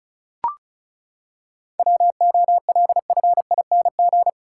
Morse code, heard as a keyed tone, about 36 wpm, 700 Hz. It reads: WOLFING
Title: Pangram Challenge v2 - 35wpm